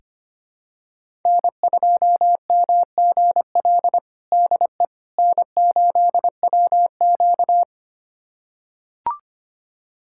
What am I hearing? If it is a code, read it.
Morse code, heard as a keyed tone, about 25 wpm, 700 Hz. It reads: N2MGL DE N8WQ
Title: USA Callsign DE Callsign 25wpm